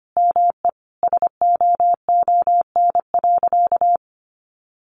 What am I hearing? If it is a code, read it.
Morse code, heard as a keyed tone, about 25 wpm, 700 Hz. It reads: ME SOON.
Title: Sentences from Top 1400 Words 25wpm